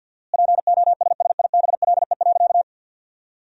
Morse code, heard as a keyed tone, about 50 wpm, 700 Hz. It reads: POSSIBLE.
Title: 4x Word-Spacing - Sentences from Top 500 Words - 50wpm